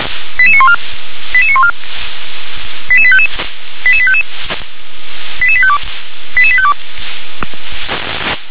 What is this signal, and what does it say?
Five-tone selective call (ZVEI), heard as quick runs of five tones, tones of about 70 ms. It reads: 88914 88914 88855 88855 88852 88852
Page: /357/SDR/ZVEI/